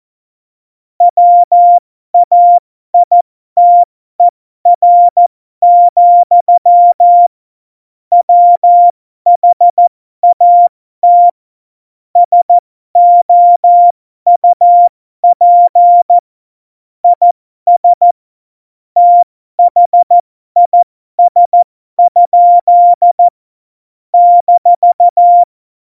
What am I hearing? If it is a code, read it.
Morse code, heard as a keyed tone, about 14 wpm, spaced as for 12 wpm, 700 Hz. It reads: WAITER, WHAT SOUP IS THIS? -
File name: Morsekurs / Witze